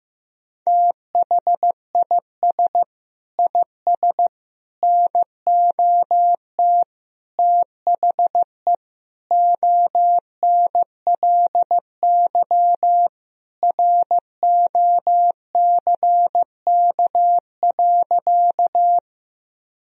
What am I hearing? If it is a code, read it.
Morse code, heard as a keyed tone, about 16 wpm, 700 Hz. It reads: THIS IS NOT THE ONLY ROCK.
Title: Sentences from Top 1000 Words - Review Part 2 - 15wpm